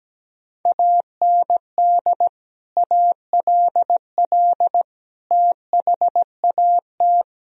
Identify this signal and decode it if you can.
Morse code, heard as a keyed tone, about 17 wpm, 700 Hz. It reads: AND ALL THAT
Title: Sentences from Top 1000 Words - Review Part 4 - 17wpm